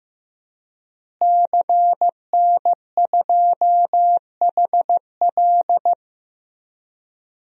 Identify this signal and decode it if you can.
Morse code, heard as a keyed tone, about 15 wpm, 700 Hz. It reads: CN2HL
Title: DXCC-100 Call Signs - 5 Characters - Repeated 3x - 15wpm